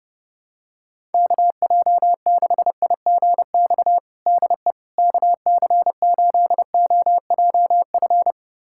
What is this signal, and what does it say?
Morse code, heard as a keyed tone, about 29 wpm, 700 Hz. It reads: KJ6IGX DE KC8OJF
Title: USA Callsign DE Callsign - 6 Characters - 30wpm